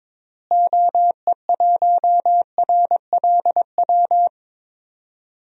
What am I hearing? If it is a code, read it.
Morse code, heard as a keyed tone, about 22 wpm, 700 Hz. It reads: OE1RLW
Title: DXCC-100 Call Signs - 6 Characters - Repeated 3x - 22wpm